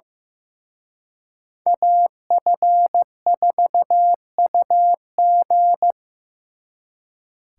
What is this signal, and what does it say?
Morse code, heard as a keyed tone, about 18 wpm, 700 Hz. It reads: AF4UG
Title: Speed Racing - USA 2x2 Call Signs 15wpm